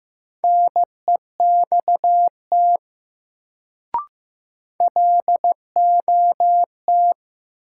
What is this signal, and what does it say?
Morse code, heard as a keyed tone, about 15 wpm, 700 Hz. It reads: NEXT LOT
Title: Sets of 2 Words - New Word Drill - 15wpm